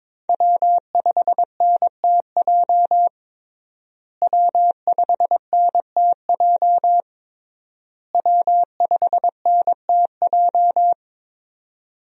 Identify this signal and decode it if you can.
Morse code, heard as a keyed tone, about 22 wpm, 700 Hz. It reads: W5NTJ W5NTJ W5NTJ
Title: USA Call Signs - 5 Characters - Repeated 3x - 22wpm